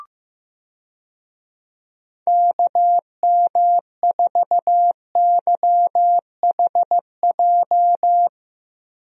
Morse code, heard as a keyed tone, about 15 wpm, 700 Hz. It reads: KM4YHJ
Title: USA Call Signs - 6 Characters - 15wpm